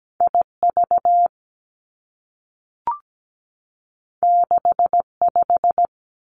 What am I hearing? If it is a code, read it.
Morse code, heard as a keyed tone, about 17 wpm, 700 Hz. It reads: IV 65
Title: Two Challenging Letter-Number Combinations 17wpm